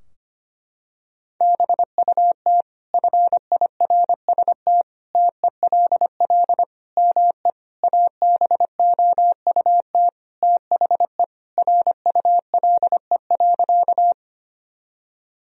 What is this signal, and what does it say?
Morse code, heard as a keyed tone, about 25 wpm, 700 Hz. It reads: BUT FIRST TELL ME ABOUT THE RULE.
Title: Sentences from Top 1000 Words - Review Part 2 - 25wpm